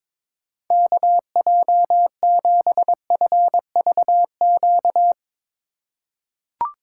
Morse code, heard as a keyed tone, about 22 wpm, 700 Hz. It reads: KJ7FVQ
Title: USA Call Signs - 6 Characters - 2x Character Spacing - 22wpm